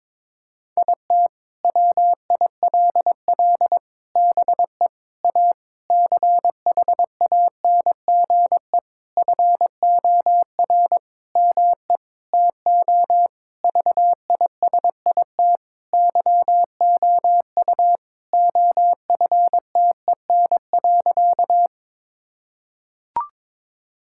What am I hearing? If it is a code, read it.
Morse code, heard as a keyed tone, about 22 wpm, 700 Hz. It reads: IT WILL BE A CHANGE FOR ME TO VISIT YOU OFTEN.
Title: Sentences from Top 1000 Words - Review Part 6 - 22wpm